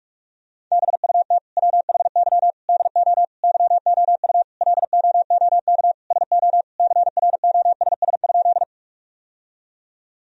Morse code, heard as a keyed tone, about 42 wpm, 700 Hz. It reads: BUT WHY DO YOU LOOK SO CROSS?
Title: Sentences from Top 1000 Words - Review Part 1 - 45wpm